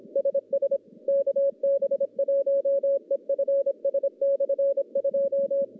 Morse code, heard as a keyed tone, about 26 wpm, 550 Hz. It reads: SS KB1EFS/2
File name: contests / CW-SS-k5tr- -1100